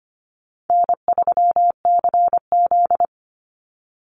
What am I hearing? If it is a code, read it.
Morse code, heard as a keyed tone, about 25 wpm, 700 Hz. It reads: N3CZ
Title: Call Signs Repeated 2x - 25wpm